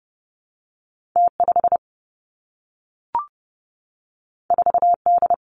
Morse code, heard as a keyed tone, about 30 wpm, 700 Hz. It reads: T 5 4 D
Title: Two Challenging Letter-Number Combinations 30wpm